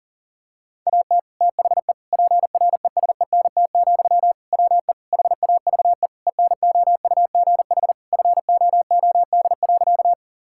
Morse code, heard as a keyed tone, about 40 wpm, 700 Hz. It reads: AT THE PRESENT, WE HAVE ENOUGH FOOD.
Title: Sentences from Top 1000 Words - Review Part 5 - 40wpm